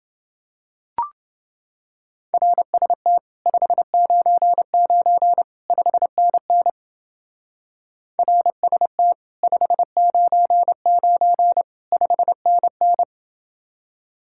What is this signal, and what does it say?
Morse code, heard as a keyed tone, about 29 wpm, 700 Hz. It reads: RST 599 5NN RST 599 5NN